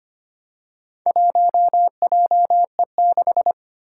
Morse code, heard as a keyed tone, about 25 wpm, 700 Hz. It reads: 1JE6
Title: Four Challenging Letter-Number Combinations 25wpm